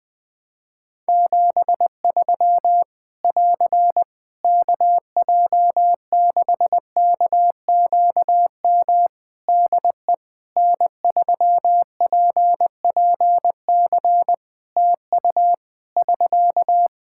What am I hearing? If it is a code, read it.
Morse code, heard as a keyed tone, about 20 wpm, 700 Hz. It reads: 73 <AR> KJ6KQM DE N3PPC TU <SK>
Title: K1USN SST QSO Practice 20wpm